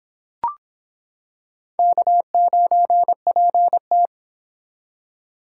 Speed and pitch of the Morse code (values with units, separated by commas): 25 wpm, 700 Hz